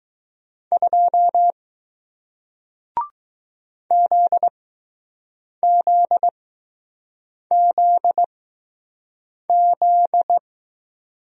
Morse code, heard as a keyed tone, about 20 wpm, spaced as for 6 wpm, 700 Hz. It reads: 2 ZZZZ